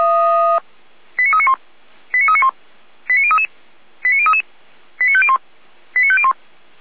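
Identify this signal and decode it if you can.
Five-tone selective call (ZVEI), heard as quick runs of five tones, tones of about 70 ms. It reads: 89391 89391 89033 89033 89501 89501
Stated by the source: Five-tone ZVEI selective call 89050, received on an SDR.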